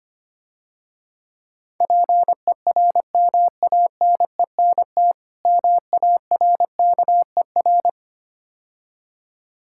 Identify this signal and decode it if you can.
Morse code, heard as a keyed tone, about 25 wpm, 690 Hz. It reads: PERMANENT MARKER
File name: Easy Words - Speed-Racing with Hints - 25wpm